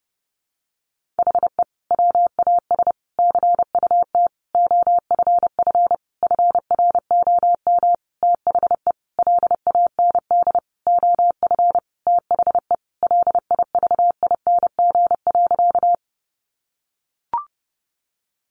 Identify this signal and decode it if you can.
Morse code, heard as a keyed tone, about 30 wpm, 700 Hz. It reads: HE WAS CUT OFF FROM THE LAND OF THE LIVING.
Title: Sentences from Top 1100 Words 30wpm